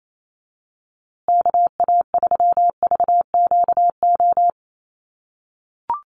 Morse code, heard as a keyed tone, about 28 wpm, 700 Hz. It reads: KA3VQO
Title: USA Call Signs Repeated 2x 28wpm